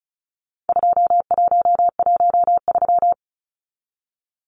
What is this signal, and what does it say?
Morse code, heard as a keyed tone, about 34 wpm, 700 Hz. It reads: 2113
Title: Numbers - 4-Digit 35wpm